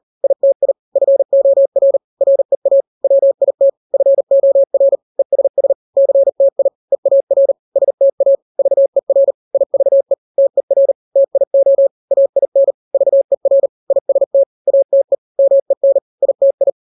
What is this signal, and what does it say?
Morse code, heard as a keyed tone, about 40 wpm, 550 Hz. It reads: ITI FOR REA WIT FOR ESS CTI EAR STA VER IVE TER TIO AIN VER IST ATE MEN ITI
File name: n-grams / 3-grams 40 35 WPM